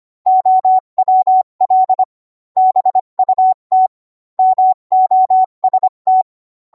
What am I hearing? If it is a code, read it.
Morse code, heard as a keyed tone, about 25 wpm, 750 Hz. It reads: OWL BUT MOST